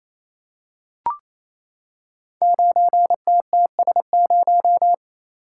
Morse code, heard as a keyed tone, about 27 wpm, 700 Hz. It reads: 9TTS0
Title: Code Groups - 5 Letter:Numbers - 28wpm